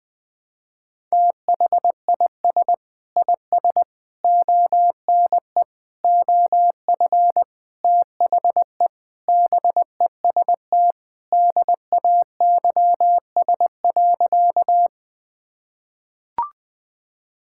Morse code, heard as a keyed tone, about 20 wpm, 700 Hz. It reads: THIS IS ONE OF THE BEST DAYS.
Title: Sentences with 5-Word Phrases - 20wpm